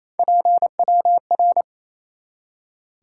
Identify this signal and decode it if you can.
Morse code, heard as a keyed tone, about 28 wpm, 700 Hz. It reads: PWR
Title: Top 180 QSO Elements 28wpm